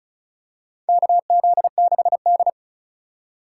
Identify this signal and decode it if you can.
Morse code, heard as a keyed tone, about 36 wpm, 700 Hz. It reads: KZ6D
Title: USA Call Signs - 4 Characters - 4x Character Spacing - 35wpm